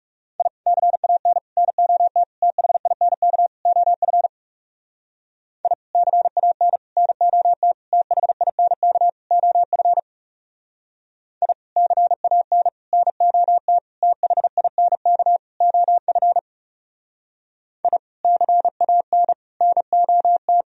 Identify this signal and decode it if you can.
Morse code, heard as a keyed tone, about 36 wpm, 700 Hz. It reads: I CAN NOT THINK OF I CAN NOT THINK OF I CAN NOT THINK OF I CAN NOT
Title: Speed-Racing - Sets of 5 Words 30wpm